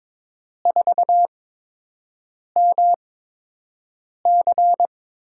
Morse code, heard as a keyed tone, about 22 wpm, spaced as for 6 wpm, 700 Hz. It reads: 4MC
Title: USA Call Signs - 8x Character Spacing - 22wpm